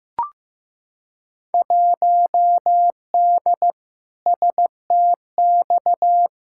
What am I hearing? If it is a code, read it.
Morse code, heard as a keyed tone, about 15 wpm, 700 Hz. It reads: 1D STX
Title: Field Day Exchanges - 15wpm